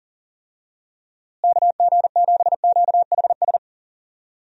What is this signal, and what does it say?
Morse code, heard as a keyed tone, about 40 wpm, 700 Hz. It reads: KG7QHS
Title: USA Call Signs - 6 Characters - Repeated 2x - 40wpm